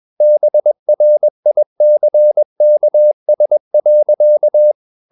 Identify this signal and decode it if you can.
Morse code, heard as a keyed tone, about 21 wpm, 600 Hz. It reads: BRICKS.
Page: CW med Gnister